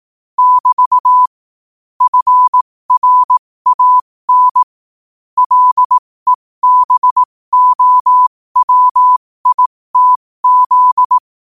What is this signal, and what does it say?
Morse code, heard as a keyed tone, about 18 wpm, spaced as for 15 wpm, 1000 Hz. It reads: <BT> FRAN LEBOWITZ
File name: qotdfiles / 15wpm